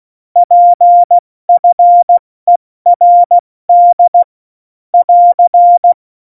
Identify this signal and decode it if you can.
Morse code, heard as a keyed tone, about 16 wpm, 700 Hz. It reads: PFERD <AR>
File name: Morsekurs / Witze / Witz 14 Juli 2020, 70 F80